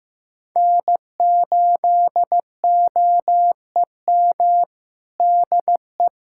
Morse code, heard as a keyed tone, about 16 wpm, 700 Hz. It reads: N8OEM DE
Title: USA Callsign DE Callsign 15wpm